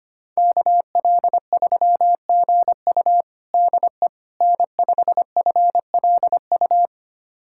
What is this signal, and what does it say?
Morse code, heard as a keyed tone, about 25 wpm, 700 Hz. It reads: KL3GU DE N5FLU
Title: USA Callsign DE Callsign - 5 Characters - 25wpm